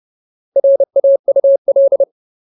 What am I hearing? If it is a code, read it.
Morse code, heard as a keyed tone, about 30 wpm, 550 Hz. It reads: RAUL
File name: Ultimate Word List / 30WPM Ultimate